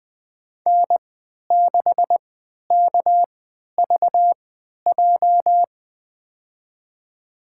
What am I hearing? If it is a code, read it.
Morse code, heard as a keyed tone, about 20 wpm, spaced as for 11 wpm, 700 Hz. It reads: N6KVJ
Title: USA Call Signs - 3x Character Spacing - 20wpm